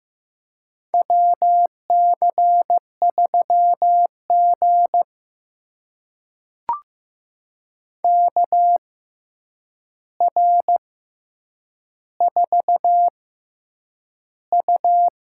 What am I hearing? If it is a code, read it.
Morse code, heard as a keyed tone, about 15 wpm, spaced as for 8 wpm, 700 Hz. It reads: WC3G KR4U
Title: USA Call Signs - 4 Characters - 6x Character Spacing - 15wpm